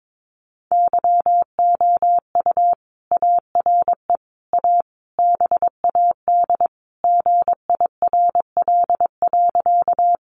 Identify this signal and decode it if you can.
Morse code, heard as a keyed tone, about 22 wpm, 700 Hz. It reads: YOU ARE A BAD GIRL.
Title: Sentences from Top 700 Words 22wpm